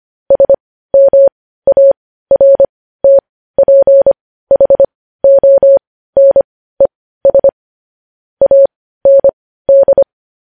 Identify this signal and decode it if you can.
Morse code, heard as a keyed tone, about 25 wpm, spaced as for 15 wpm, 550 Hz. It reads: SMARTPHONES AND
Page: News Headlines in Morse Code at 15 WPM